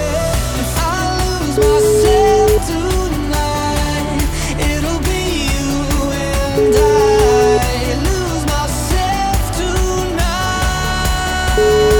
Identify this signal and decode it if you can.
Ringback tone (Europe, 425 Hz), with noise.